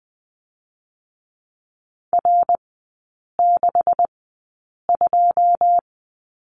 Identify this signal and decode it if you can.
Morse code, heard as a keyed tone, about 20 wpm, spaced as for 8 wpm, 700 Hz. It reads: R62
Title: 127 Lesson 21e - Number 6 - Three Characters 20wpm